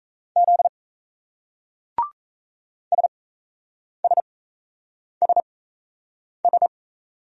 Morse code, heard as a keyed tone, about 34 wpm, spaced as for 8 wpm, 700 Hz. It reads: Z SSSS
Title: Speed-Racing Single Letters 28wpm